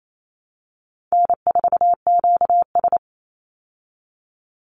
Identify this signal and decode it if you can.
Morse code, heard as a keyed tone, about 28 wpm, 700 Hz. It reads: N4QS
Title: Easy Call Signs Repeated 2x - 28wpm